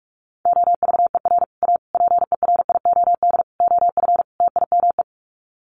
Morse code, heard as a keyed tone, about 44 wpm, 700 Hz. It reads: OVER A PERIOD OF TIME
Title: Sets of 5 Words Encore2 45wpm